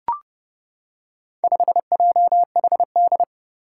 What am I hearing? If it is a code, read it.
Morse code, heard as a keyed tone, about 30 wpm, 700 Hz. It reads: 5JHD
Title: Four Challenging Letter-Number Combinations 30wpm